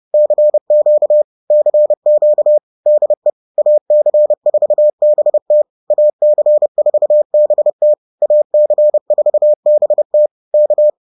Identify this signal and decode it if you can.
Morse code, heard as a keyed tone, about 30 wpm, 600 Hz. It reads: CQ CQ DE AC4BT AC4BT AC4BT K
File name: 30wpm-4